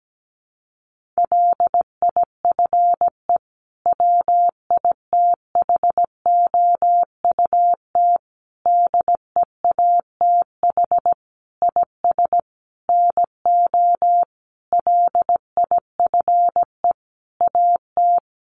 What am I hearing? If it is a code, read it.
Morse code, heard as a keyed tone, about 17 wpm, 700 Hz. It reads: LIFE WITHOUT DEATH IS NO LIFE AT
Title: Sentences from Top 1100 Words 17wpm